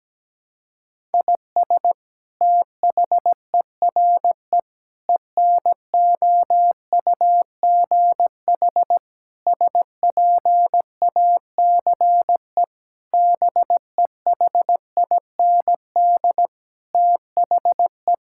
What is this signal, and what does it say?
Morse code, heard as a keyed tone, about 17 wpm, 700 Hz. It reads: IS THERE ENOUGH SPACE BEHIND THE
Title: Sentences from Top 1000 Words - Review Part 5 - 17wpm